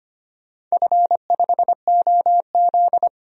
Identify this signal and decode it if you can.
Morse code, heard as a keyed tone, about 25 wpm, 700 Hz. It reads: F5OZ
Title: DXCC-100 Call Signs - 4 Characters - 25wpm